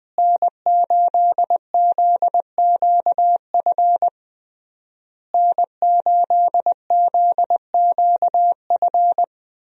Morse code, heard as a keyed tone, about 21 wpm, 700 Hz. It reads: N8ZQF N8ZQF
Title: USA Call Signs - 5 Characters - Repeated 3x - 20wpm